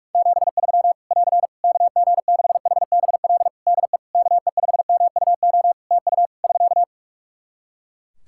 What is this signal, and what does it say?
Morse code, heard as a keyed tone, about 46 wpm, 700 Hz. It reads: 73 <AR> KG6HBL DE KE5MUO TU <SK>
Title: Practice QSO - Exchange 4 - Ending - 45wpm